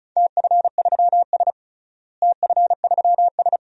Morse code, heard as a keyed tone, about 36 wpm, 700 Hz. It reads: TF3S TF3S
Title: DXCC-100 Call Signs - 4 Characters - Repeated 2x - 35wpm